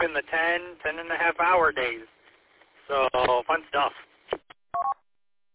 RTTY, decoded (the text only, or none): none